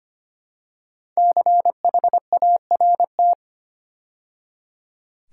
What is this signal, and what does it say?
Morse code, heard as a keyed tone, about 25 wpm, 700 Hz. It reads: CHART
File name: 5 Letter Words - First 3 Letters Given - 25wpm